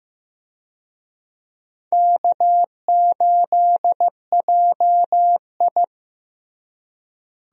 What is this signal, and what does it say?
Morse code, heard as a keyed tone, about 15 wpm, 700 Hz. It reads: K8JI
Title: USA Call Signs - 4 Characters - Repeated 3x - 15wpm